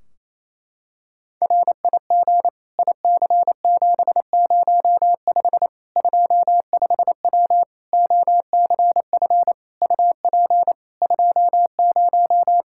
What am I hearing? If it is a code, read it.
Morse code, heard as a keyed tone, about 28 wpm, 700 Hz. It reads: RIG IC705 25W OCF UP 20
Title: K1USN SST QSO Practice 28wpm